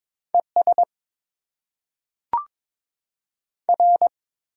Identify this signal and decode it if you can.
Morse code, heard as a keyed tone, about 22 wpm, 700 Hz. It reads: ES R
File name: Top 20 QSO Elements - 8x Character Spacing - 22wpm